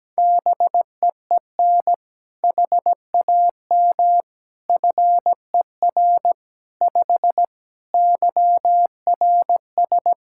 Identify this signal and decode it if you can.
Morse code, heard as a keyed tone, about 17 wpm, 700 Hz. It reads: BEEN HAM FER 5 YRS
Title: Practice QSO - Exchange 3 - More Info - 17wpm